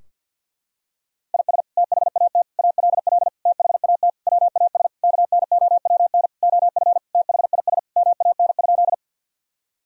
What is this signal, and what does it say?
Morse code, heard as a keyed tone, about 49 wpm, 700 Hz. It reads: IS THAT ALL THAT WAS KNOWN OF THIS MAN?
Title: Sentences from Top 1000 Words - Review Part 3 - 50wpm